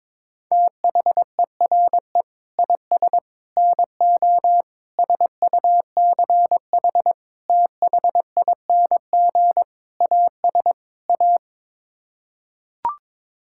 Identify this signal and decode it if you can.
Morse code, heard as a keyed tone, about 22 wpm, 700 Hz. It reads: THERE IS NO SUCH THING AS A
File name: practice / 22wpm / Sets of 7 Words - 22wpm